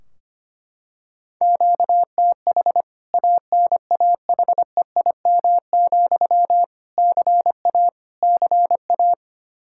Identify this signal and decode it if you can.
Morse code, heard as a keyed tone, about 25 wpm, 700 Hz. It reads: QTH ANAHEIM, CA CA
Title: K1USN SST QSO Practice 25wpm